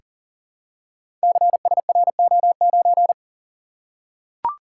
Morse code, heard as a keyed tone, about 39 wpm, 690 Hz. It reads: CSRO9
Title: Code Groups - 5 Letter:Numbers - 40wpm